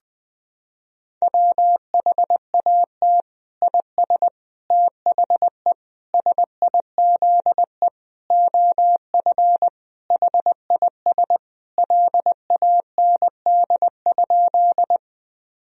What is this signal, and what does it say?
Morse code, heard as a keyed tone, about 20 wpm, 700 Hz. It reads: WHAT IS THE SIZE OF HIS LAND?
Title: Sentences from Top 1000 Words - Review Part 2 - 20wpm